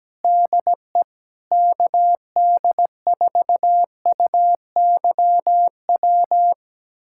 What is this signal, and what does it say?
Morse code, heard as a keyed tone, about 17 wpm, 700 Hz. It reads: DE KD4UYW
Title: USA Callsign DE Callsign - 6 Characters - 17wpm